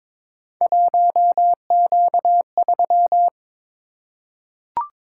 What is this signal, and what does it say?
Morse code, heard as a keyed tone, about 22 wpm, 700 Hz. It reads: 1Q3
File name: practice / 22wpm / Three Letter-Number Combinations - 22wpm